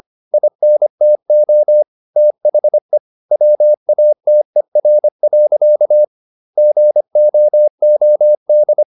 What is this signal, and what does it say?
Morse code, heard as a keyed tone, about 25 wpm, 600 Hz. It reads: INTO THE WATER. GOOD
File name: the old man and the sea part 1 25wpm